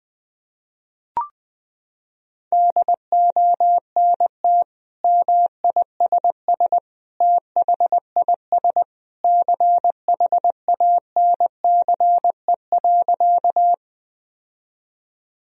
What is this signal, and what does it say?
Morse code, heard as a keyed tone, about 20 wpm, 700 Hz. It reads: DONT MISS THIS CHANCE.
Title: Sentences from Top 1000 Words - Review Part 3 - 20wpm